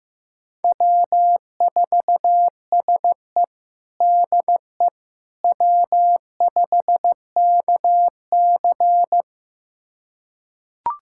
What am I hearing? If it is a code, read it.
Morse code, heard as a keyed tone, about 15 wpm, 700 Hz. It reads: W4SE DE W5KC
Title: USA Callsign DE Callsign - 4 Characters - 15wpm